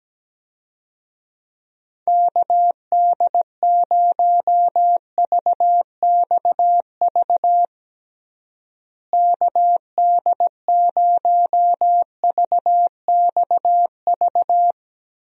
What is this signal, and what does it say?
Morse code, heard as a keyed tone, about 17 wpm, 700 Hz. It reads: KD0VXV KD0VXV
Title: USA Call Signs - 6 Characters - Repeated 3x - 17wpm